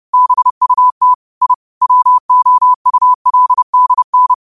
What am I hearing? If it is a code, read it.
Morse code, heard as a keyed tone, about 30 wpm, 1000 Hz. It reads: BUT I WOULDN
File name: qotdfiles / 30wpm